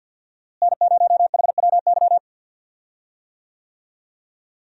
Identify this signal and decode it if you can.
Morse code, heard as a keyed tone, about 50 wpm, 700 Hz. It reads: N0HWY
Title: USA Call Signs - 5 Characters - Repeated 3x - 50wpm